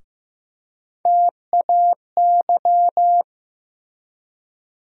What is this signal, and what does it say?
Morse code, heard as a keyed tone, about 15 wpm, 700 Hz. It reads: TAY
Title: USA Call Sign Postfixes - 3 Letters 15wpm